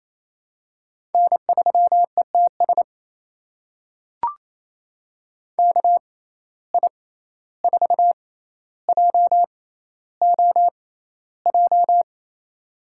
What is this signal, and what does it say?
Morse code, heard as a keyed tone, about 28 wpm, spaced as for 14 wpm, 700 Hz. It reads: N3ETS KI4JOJ
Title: USA Call Signs - 6x Character Spacing - 28wpm